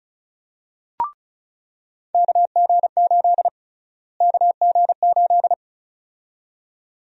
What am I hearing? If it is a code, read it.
Morse code, heard as a keyed tone, about 34 wpm, 700 Hz. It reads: KG8 KG8
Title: USA Call Sign Prefixes - Repeated 2x - 35wpm